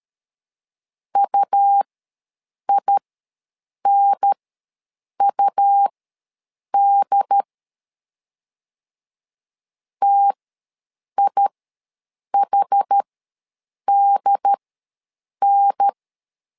Morse code, heard as a keyed tone, about 13 wpm, spaced as for 7 wpm, 800 Hz. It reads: UINUD TIHDN
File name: LEZIONI / 2 UHDN / 2 UHDN ws 12wpm 1708